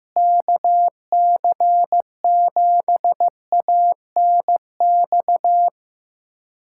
Morse code, heard as a keyed tone, about 15 wpm, 700 Hz. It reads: KC7ANX
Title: USA Callsign DE Callsign 15wpm